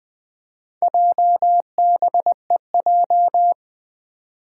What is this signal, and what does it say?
Morse code, heard as a keyed tone, about 20 wpm, 700 Hz. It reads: JBEJ
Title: Four Challenging Letter-Number Combinations 20wpm